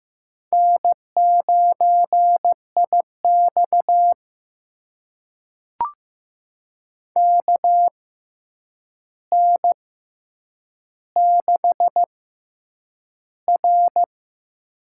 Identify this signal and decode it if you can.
Morse code, heard as a keyed tone, about 15 wpm, spaced as for 8 wpm, 700 Hz. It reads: N9IX KN6R
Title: USA Call Signs - 4 Characters - 6x Character Spacing - 15wpm